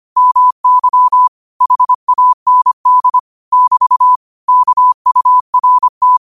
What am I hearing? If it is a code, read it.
Morse code, heard as a keyed tone, about 25 wpm, 1000 Hz. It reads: MY HAND <BT> KURT
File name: qotdfiles / 25wpm